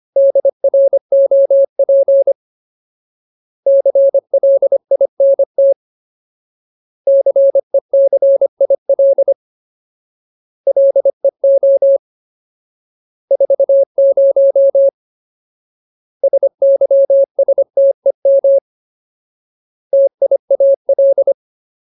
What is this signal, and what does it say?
Morse code, heard as a keyed tone, about 25 wpm, 550 Hz. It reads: DROP CLINT CECIL LEO 40 SYSTEM TIAL
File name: Ultimate Word List / 25WPM Ultimate